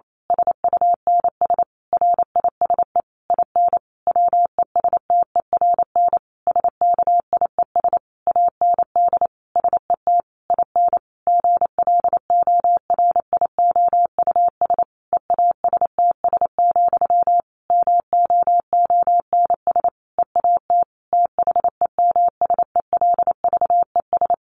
Morse code, heard as a keyed tone, about 28 wpm, 700 Hz. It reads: SUNS RISE IN WESTERN SKIES AND SET IN GLORIOUS EASTS, MOONS EAT THEMSELVES
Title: A Sound Of Thunder 28wpm